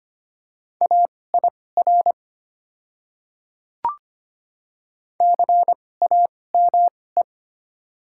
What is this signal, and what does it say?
Morse code, heard as a keyed tone, about 25 wpm, spaced as for 18 wpm, 700 Hz. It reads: AIR CAME
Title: 2x Character Spacing - Top 200 Words - 25wpm